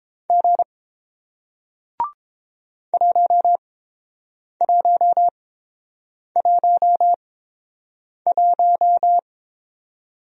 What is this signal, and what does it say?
Morse code, heard as a keyed tone, about 30 wpm, spaced as for 7 wpm, 700 Hz. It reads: G 1111